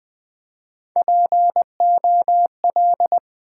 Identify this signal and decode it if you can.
Morse code, heard as a keyed tone, about 20 wpm, 700 Hz. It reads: POL
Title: Build A Word - 3 Letters At A Time - 8-Letter Words - 20wpm